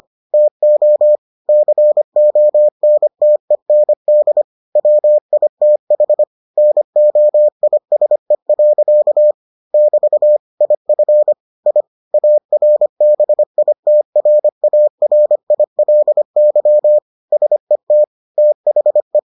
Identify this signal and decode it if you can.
Morse code, heard as a keyed tone, about 25 wpm, 600 Hz. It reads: TO CONTEND WITH NOISE. <BT> IF I ARBITRARILY SET THE